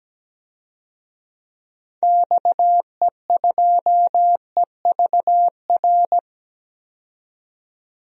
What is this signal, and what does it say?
Morse code, heard as a keyed tone, about 17 wpm, 700 Hz. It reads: XE2EVR
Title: DXCC-100 Call Signs - 6 Characters - 17wpm